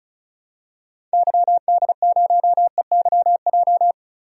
Morse code, heard as a keyed tone, about 36 wpm, 700 Hz. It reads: YD0EYJ
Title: DXCC-100 Call Signs - 6 Characters - Repeated 2x - 35wpm